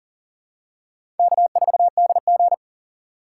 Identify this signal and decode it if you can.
Morse code, heard as a keyed tone, about 40 wpm, 700 Hz. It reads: K4DG
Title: USA Call Signs - 4 Characters - 3x Character Spacing - 40wpm